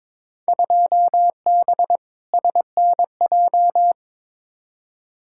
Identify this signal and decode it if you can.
Morse code, heard as a keyed tone, about 22 wpm, 700 Hz. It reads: 2B SNJ
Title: Field Day Exchanges Encore - 22wpm